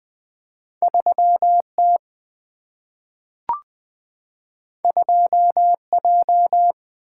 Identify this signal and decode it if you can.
Morse code, heard as a keyed tone, about 20 wpm, 700 Hz. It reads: 3T 2J
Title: Single-Letter Single-Number 20wpm